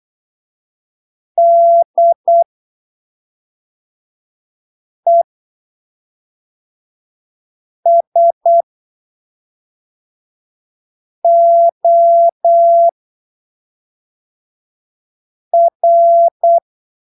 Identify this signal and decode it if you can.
Morse code, heard as a keyed tone, about 8 wpm, spaced as for 3 wpm, 680 Hz. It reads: DESOR
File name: arquivos / texto / texto 05 08ppm